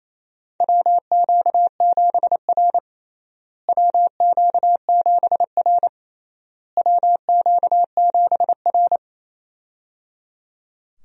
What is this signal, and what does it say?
Morse code, heard as a keyed tone, about 28 wpm, 700 Hz. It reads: WQ7R WQ7R WQ7R
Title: USA Call Signs - 4 Characters - Repeated 3x - 28wpm